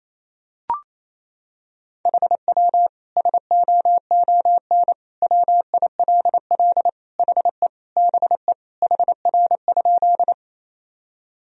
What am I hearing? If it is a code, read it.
Morse code, heard as a keyed tone, about 28 wpm, 700 Hz. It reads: HW SOON WILL HE BE HR?
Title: Sentences from Top 400 Words and Abbreviations 28wpm